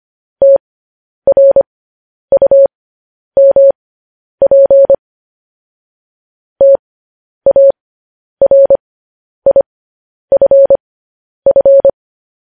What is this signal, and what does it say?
Morse code, heard as a keyed tone, about 25 wpm, spaced as for 10 wpm, 550 Hz. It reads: TRUMP TARIFF
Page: news headlines in morse code at 10 wpm